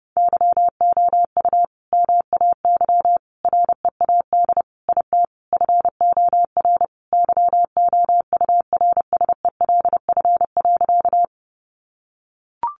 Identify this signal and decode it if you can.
Morse code, heard as a keyed tone, about 30 wpm, 700 Hz. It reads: YOU MAY READ IT FOR YOURSELF.
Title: Sentences from Top 600 Words 30wpm